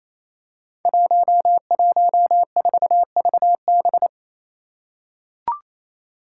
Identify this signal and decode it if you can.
Morse code, heard as a keyed tone, about 28 wpm, 700 Hz. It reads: 114VB
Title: Five Challenging Letter-Number Combinations 28wpm